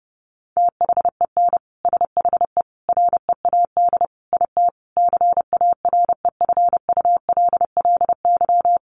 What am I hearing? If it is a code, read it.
Morse code, heard as a keyed tone, about 30 wpm, 700 Hz. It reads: THEN SHE READ IT CAREFULLY
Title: Sentences from Top 1300 Words 30wpm